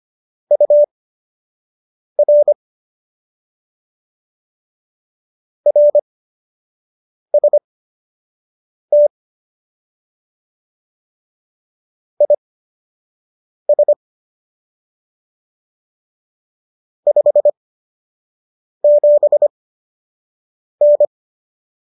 Morse code, heard as a keyed tone, about 25 wpm, spaced as for 6 wpm, 590 Hz. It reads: UR RST IS 57N